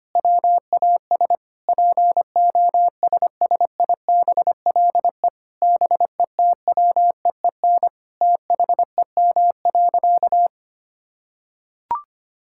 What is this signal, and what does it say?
Morse code, heard as a keyed tone, about 25 wpm, 700 Hz. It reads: WAS POSSIBLE BETWEEN THEM.
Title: Sentences from Top 1000 Words - Review Part 5 - 25wpm